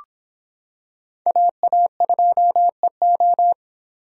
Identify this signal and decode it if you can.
Morse code, heard as a keyed tone, about 25 wpm, 700 Hz. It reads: AA2EO